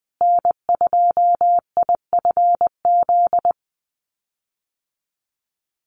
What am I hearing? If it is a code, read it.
Morse code, heard as a keyed tone, about 20 wpm, 700 Hz. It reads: N2IFZ
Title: USA Call Signs Repeated 3x 20wpm